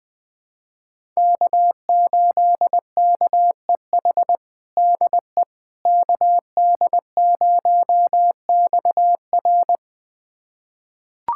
Morse code, heard as a keyed tone, about 20 wpm, 700 Hz. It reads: K8KEH DE KD0XR
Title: USA Callsign DE Callsign - 5 Characters - 20wpm